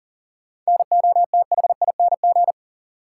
Morse code, heard as a keyed tone, about 40 wpm, 700 Hz. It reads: NOTHING
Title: Sentences from Top 500 Words - New Word Drill - 40wpm